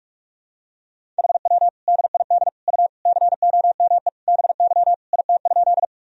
Morse code, heard as a keyed tone, about 46 wpm, 700 Hz. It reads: HW DID U COME BY IT?
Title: Sentences from Top 100 Words and Abbreviations 45wpm